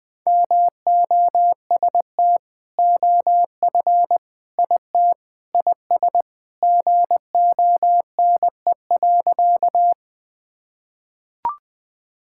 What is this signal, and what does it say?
Morse code, heard as a keyed tone, about 20 wpm, 700 Hz. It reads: MOST OF IT IS GONE.
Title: Sentences with 2-Word Phrases - 20wpm